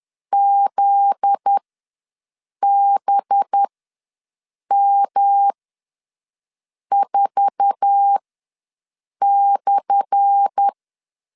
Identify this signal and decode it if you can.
Morse code, heard as a keyed tone, about 10 wpm, spaced as for 6 wpm, 800 Hz. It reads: ZBM4/
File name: LEZIONI / 10 SIGNS / 10A SLASH ws 10wpm 4121